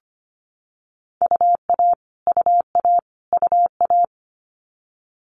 Morse code, heard as a keyed tone, about 25 wpm, 700 Hz. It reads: UA UA UA
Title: Multi-letter Phonemes Repeated 3x 25wpm